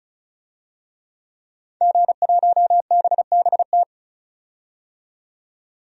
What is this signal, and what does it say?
Morse code, heard as a keyed tone, about 34 wpm, 700 Hz. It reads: G1BBT
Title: DXCC-100 Call Signs - 5 Characters - 35wpm